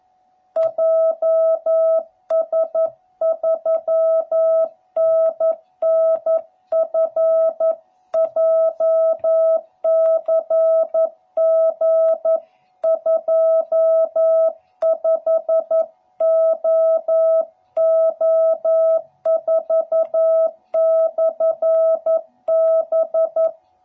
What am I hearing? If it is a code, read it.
Morse code, heard as a keyed tone, about 11 wpm, 660 Hz. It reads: JS3NNF JCG25OO4/B
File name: jh8cbh / 2021cw / cw12b